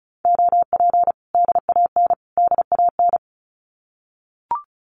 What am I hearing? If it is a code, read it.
Morse code, heard as a keyed tone, about 34 wpm, 700 Hz. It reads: OP DAN DAN
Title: Common QSO Phrases Encore 35wpm